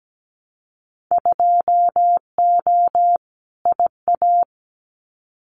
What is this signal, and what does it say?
Morse code, heard as a keyed tone, about 17 wpm, 700 Hz. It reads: 2O IA
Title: Winter Field Day Exchanges 17wpm